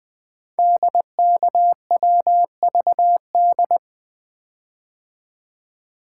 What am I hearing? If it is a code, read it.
Morse code, heard as a keyed tone, about 20 wpm, 700 Hz. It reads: DKWVD
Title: Code Groups - 5 Letters - 20wpm